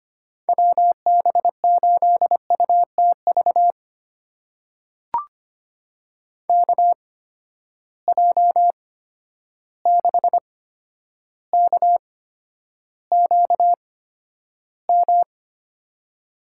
Morse code, heard as a keyed tone, about 25 wpm, spaced as for 11 wpm, 700 Hz. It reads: WB8UTV KJ6KQM
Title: USA Call Signs - 6 Characters - 8x Character Spacing - 25wpm